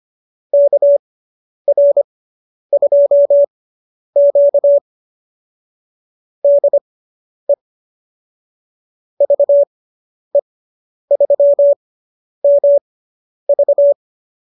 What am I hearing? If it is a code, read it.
Morse code, heard as a keyed tone, about 25 wpm, spaced as for 10 wpm, 590 Hz. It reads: KR2Q DE VE3MV
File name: QSO220 10